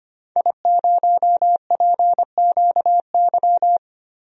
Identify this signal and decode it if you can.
Morse code, heard as a keyed tone, about 25 wpm, 700 Hz. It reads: I0PQY
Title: DXCC-100 Call Signs - 5 Characters - 25wpm